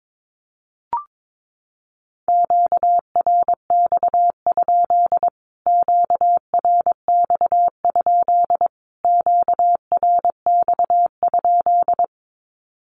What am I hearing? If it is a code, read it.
Morse code, heard as a keyed tone, about 22 wpm, 700 Hz. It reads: QRX? QRX? QRX?
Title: Q-Codes Repeated 3x 22wpm